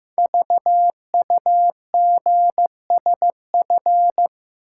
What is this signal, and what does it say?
Morse code, heard as a keyed tone, about 15 wpm, 700 Hz. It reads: VUGSF
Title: Code Groups - 5 Letters - 15wpm